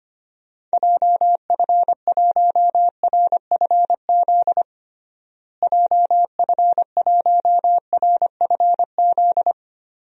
Morse code, heard as a keyed tone, about 25 wpm, 700 Hz. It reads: JF1RFZ JF1RFZ
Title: DXCC-100 Call Signs - 6 Characters - Repeated 2x - 25wpm